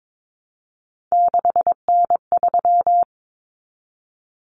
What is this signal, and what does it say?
Morse code, heard as a keyed tone, about 22 wpm, 700 Hz. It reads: 6N3
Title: Common Cut Numbers - 3-Digit 22wpm